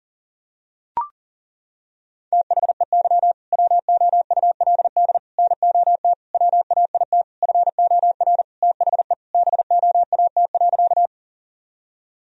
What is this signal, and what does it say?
Morse code, heard as a keyed tone, about 40 wpm, 700 Hz. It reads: THEY WOULD NOT WAIT FOR THE BOAT.
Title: Sentences from Top 1000 Words - Review Part 3 - 40wpm